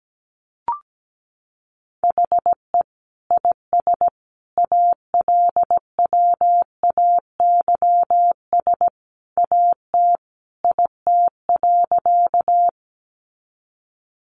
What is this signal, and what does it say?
Morse code, heard as a keyed tone, about 17 wpm, 700 Hz. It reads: HE IS ALWAYS AT IT.
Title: Sentences from Top 300 Words 17wpm